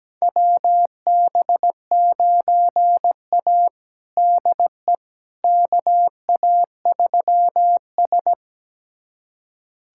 Morse code, heard as a keyed tone, about 17 wpm, 700 Hz. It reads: WB9A DE KA3S
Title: USA Callsign DE Callsign - 4 Characters - 17wpm